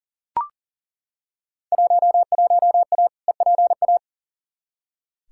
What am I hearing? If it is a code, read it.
Morse code, heard as a keyed tone, about 40 wpm, 700 Hz. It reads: 11A EPA
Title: Field Day Exchanges Encore - 40wpm